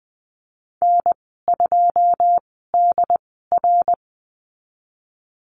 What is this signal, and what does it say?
Morse code, heard as a keyed tone, about 20 wpm, spaced as for 14 wpm, 700 Hz. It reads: N2DR
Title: 088 Lesson 15g - Number 5 - Call Signs 20wpm